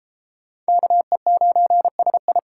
Morse code, heard as a keyed tone, about 32 wpm, 700 Hz. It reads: KE9SI